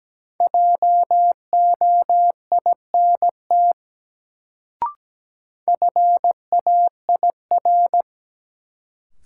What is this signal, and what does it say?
Morse code, heard as a keyed tone, about 17 wpm, 700 Hz. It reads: JOINT FAIR
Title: Sentences from Top 1300 Words - New Word Drill Alt - 17wpm